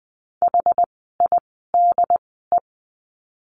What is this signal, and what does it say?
Morse code, heard as a keyed tone, about 20 wpm, spaced as for 15 wpm, 700 Hz. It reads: HIDE
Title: 068 Lesson 12f - Letter d - Words 20wpm